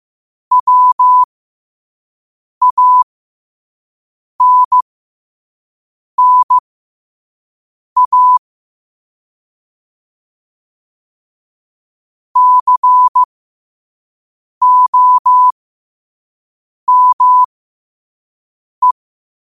Morse code, heard as a keyed tone, about 15 wpm, spaced as for 5 wpm, 1000 Hz. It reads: WANNA COME
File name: qotdfiles / 5wpm